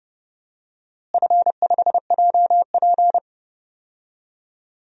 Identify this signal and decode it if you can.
Morse code, heard as a keyed tone, about 30 wpm, 700 Hz. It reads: F5JP
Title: DXCC-100 Call Signs - 4 Characters - Repeated 2x - 30wpm